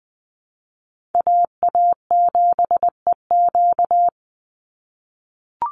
Morse code, heard as a keyed tone, about 20 wpm, 700 Hz. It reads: AA7EQ
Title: NA SOTA Chaser Callsigns 20wpm